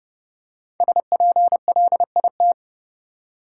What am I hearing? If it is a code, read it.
Morse code, heard as a keyed tone, about 30 wpm, 700 Hz. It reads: SPLIT
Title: Five Letter Words - 30wpm